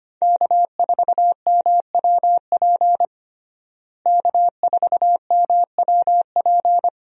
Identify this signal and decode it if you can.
Morse code, heard as a keyed tone, about 25 wpm, 700 Hz. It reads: K4MWP K4MWP
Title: USA Call Signs - 5 Characters - Repeated 3x - 25wpm